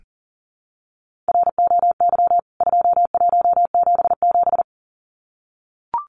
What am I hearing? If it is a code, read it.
Morse code, heard as a keyed tone, about 40 wpm, 700 Hz. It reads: ROY 2177
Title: CWT Exchange 40wpm